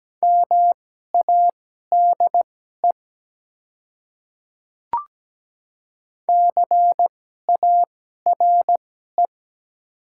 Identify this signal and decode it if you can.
Morse code, heard as a keyed tone, about 17 wpm, spaced as for 12 wpm, 690 Hz. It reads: MADE CARE
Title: 2x Character Spacing - Top 300 Words - 17wpm